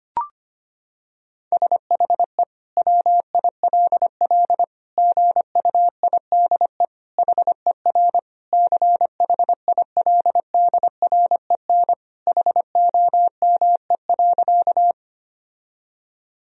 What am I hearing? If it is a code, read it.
Morse code, heard as a keyed tone, about 25 wpm, 700 Hz. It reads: SHE WILL GUIDE HER CHILDREN HOME.
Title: Sentences from Top 1000 Words - Review Part 3 - 25wpm